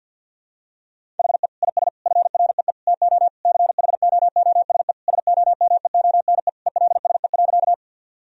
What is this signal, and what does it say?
Morse code, heard as a keyed tone, about 50 wpm, 700 Hz. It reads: HE IS FREE TO CHOOSE SOMEONE ELSE.
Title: Sentences from Top 1000 Words - Review Part 4 - 50wpm